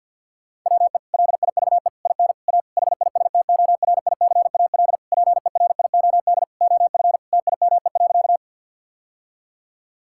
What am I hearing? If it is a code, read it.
Morse code, heard as a keyed tone, about 50 wpm, 700 Hz. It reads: WE LIVE IN A HISTORICAL PERIOD OF TIME.
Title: Sentences from Top 1000 Words - Review Part 5 - 50wpm